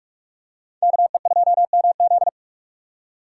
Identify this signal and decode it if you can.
Morse code, heard as a keyed tone, about 44 wpm, 700 Hz. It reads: KE2MZ
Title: USA Call Signs - 5 Characters - 4x Character Spacing - 45wpm